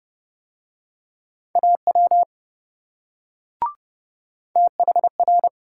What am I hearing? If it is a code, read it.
Morse code, heard as a keyed tone, about 27 wpm, 700 Hz. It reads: AW THR